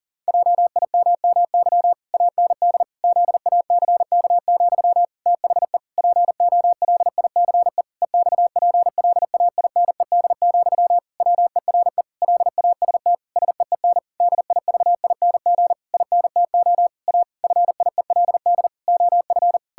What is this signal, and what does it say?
Morse code, heard as a keyed tone, about 40 wpm, 700 Hz. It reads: JIMMY AND ZACK, THE POLICE EXPLAINED, WERE LAST SEEN DIVING INTO A FIELD OF
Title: Pangram Challenge v2 - 40wpm